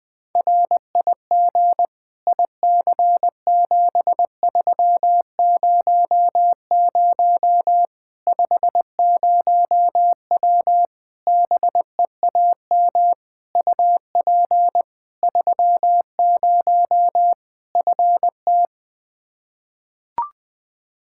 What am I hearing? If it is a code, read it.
Morse code, heard as a keyed tone, about 20 wpm, 700 Hz. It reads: RIG IC7300 50W BEAM UP 30 FT
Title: Practice QSO - Exchange 2 - Info - 20wpm